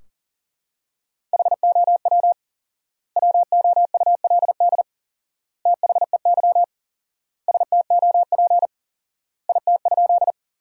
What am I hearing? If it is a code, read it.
Morse code, heard as a keyed tone, about 40 wpm, 700 Hz. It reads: HOW WOULD THEY STOP IT?
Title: 4x Word-Spacing - Sentences from Top 300 Words - 40wpm